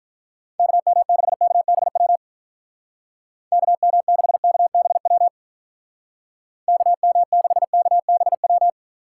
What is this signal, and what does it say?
Morse code, heard as a keyed tone, about 44 wpm, 700 Hz. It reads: KM6KBW KM6KBW KM6KBW